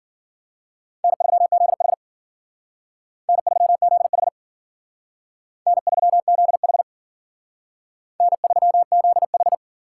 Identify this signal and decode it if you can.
Morse code, heard as a keyed tone, about 52 wpm, 700 Hz. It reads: N3ZH N3ZH N3ZH N3ZH